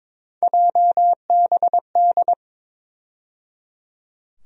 Morse code, heard as a keyed tone, about 22 wpm, 700 Hz. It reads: JBD
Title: USA Call Sign Postfixes - 3 Letters 22wpm